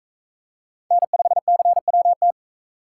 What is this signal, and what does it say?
Morse code, heard as a keyed tone, about 38 wpm, 700 Hz. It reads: N5CWT